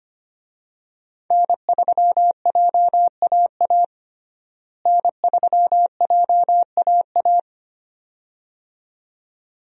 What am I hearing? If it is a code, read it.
Morse code, heard as a keyed tone, about 25 wpm, 700 Hz. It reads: N3JAA N3JAA
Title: USA Call Signs - 5 Characters - Repeated 2x - 25wpm